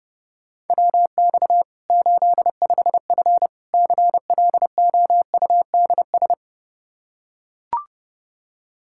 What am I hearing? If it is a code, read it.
Morse code, heard as a keyed tone, about 30 wpm, 700 Hz. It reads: WX 85F CLOUDS
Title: Practice QSO - Exchange 2 - Info - 30wpm